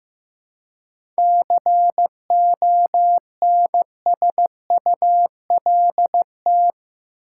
Morse code, heard as a keyed tone, about 15 wpm, 700 Hz. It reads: CONSULT
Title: Seven Letter Words - 15wpm